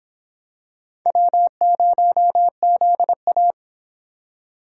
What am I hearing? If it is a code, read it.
Morse code, heard as a keyed tone, about 25 wpm, 700 Hz. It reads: W0ZA